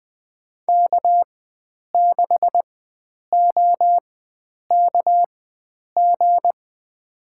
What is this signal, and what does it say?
Morse code, heard as a keyed tone, about 20 wpm, spaced as for 9 wpm, 700 Hz. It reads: K6OKG
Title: USA Call Signs - 5 Characters - 4x Character Spacing - 20wpm